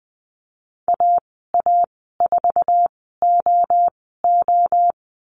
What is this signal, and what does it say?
Morse code, heard as a keyed tone, about 20 wpm, spaced as for 15 wpm, 700 Hz. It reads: AA4OO
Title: 048 Lesson 9g - Number 4 - Call Signs 20wpm